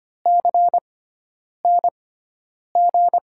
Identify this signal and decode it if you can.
Morse code, heard as a keyed tone, about 25 wpm, spaced as for 9 wpm, 700 Hz. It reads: CNG
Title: USA Call Signs - 6 Characters - 6x Character Spacing - 25wpm